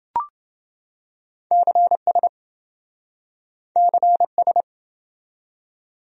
Morse code, heard as a keyed tone, about 27 wpm, 700 Hz. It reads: CS CS